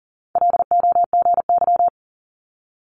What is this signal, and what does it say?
Morse code, heard as a keyed tone, about 40 wpm, 700 Hz. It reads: LOGY
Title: Multi-letter Phonemes Quick Fire 40wpm